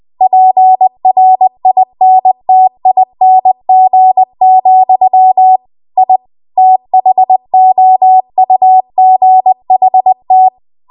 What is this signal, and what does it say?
Morse code, heard as a keyed tone, about 20 wpm, 750 Hz. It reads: PRINTING, I THOUGHT
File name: Archive / 20 WPM / 20WPM